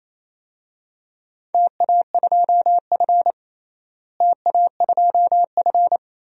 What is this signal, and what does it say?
Morse code, heard as a keyed tone, about 28 wpm, 700 Hz. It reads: TA2F TA2F
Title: DXCC-100 Call Signs - 4 Characters - Repeated 2x - 28wpm